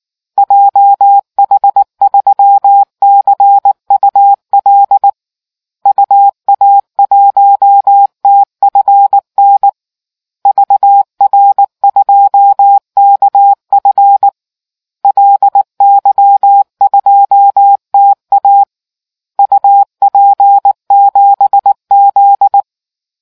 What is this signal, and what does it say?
Morse code, heard as a keyed tone, about 19 wpm, 800 Hz. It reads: JH3CUL UA1TFN VR2KF LY2TA UP7Z